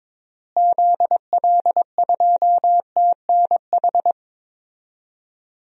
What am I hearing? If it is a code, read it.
Morse code, heard as a keyed tone, about 22 wpm, 700 Hz. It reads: ZL2TNH
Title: DXCC-100 Call Signs - 6 Characters - Repeated 2x - 22wpm